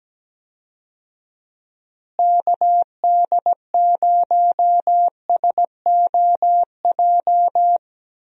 Morse code, heard as a keyed tone, about 17 wpm, 700 Hz. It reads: KD0SOJ
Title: USA Call Signs - 6 Characters - Repeated 2x - 17wpm